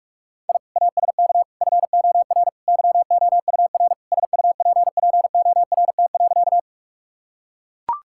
Morse code, heard as a keyed tone, about 41 wpm, 700 Hz. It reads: I ASK FOR YOUR SUPPORT.
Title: Sentences from Top 1000 Words - Review Part 5 - 45wpm